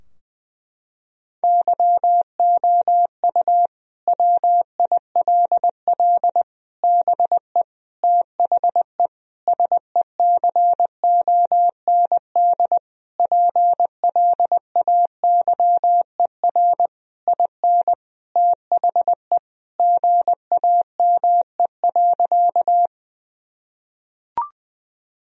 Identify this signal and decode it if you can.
Morse code, heard as a keyed tone, about 20 wpm, 700 Hz. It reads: YOU WILL BE THE SECOND PLAYER IN THE GAME.
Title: Sentences from Top 1000 Words - Review Part 4 - 20wpm